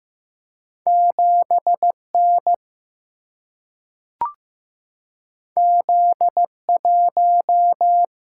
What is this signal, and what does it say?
Morse code, heard as a keyed tone, about 15 wpm, 700 Hz. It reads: 7N Z1
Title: Single-Letter Single-Number 15wpm